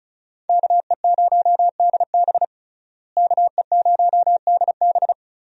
Morse code, heard as a keyed tone, about 36 wpm, 700 Hz. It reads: KE0DB KE0DB
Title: USA Call Signs - 5 Characters - Repeated 3x - 35wpm